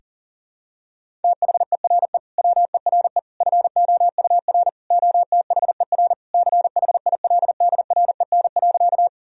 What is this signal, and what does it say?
Morse code, heard as a keyed tone, about 39 wpm, 700 Hz. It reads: THERE WERE FOUR OTHER CHILDREN.
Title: Sentences from Top 1000 Words - Review Part 1 - 40wpm